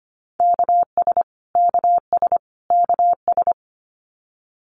Morse code, heard as a keyed tone, about 25 wpm, 700 Hz. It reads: KS KS KS
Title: US States Repeated 3x 25wpm